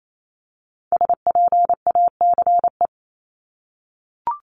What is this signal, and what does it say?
Morse code, heard as a keyed tone, about 28 wpm, 700 Hz. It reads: SPACE
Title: Multi-letter Phonemes Repeated 3x 28wpm